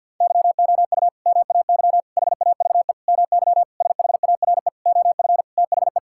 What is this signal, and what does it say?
Morse code, heard as a keyed tone, about 50 wpm, 700 Hz. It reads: YOU MAY HAVE MY SHARE OF THE
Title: Sentences from Top 1000 Words - Review Part 6 - 50wpm